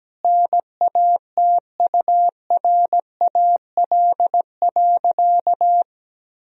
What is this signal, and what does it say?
Morse code, heard as a keyed tone, about 17 wpm, 700 Hz. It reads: NATURAL.
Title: Sentences from Top 1000 Words - Review Part 4 - 17wpm